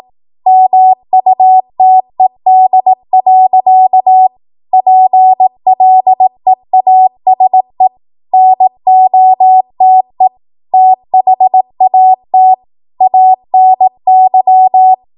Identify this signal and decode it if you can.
Morse code, heard as a keyed tone, about 18 wpm, 750 Hz. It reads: MUTED. PLEASE NOTE THAT ANY
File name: Archive / 18 WPM / 18WPM